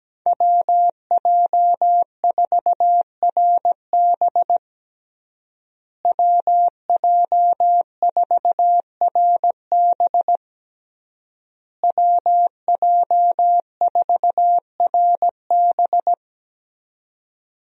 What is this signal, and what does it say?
Morse code, heard as a keyed tone, about 17 wpm, 700 Hz. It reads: WJ4RB WJ4RB WJ4RB
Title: USA Call Signs - 5 Characters - Repeated 3x - 17wpm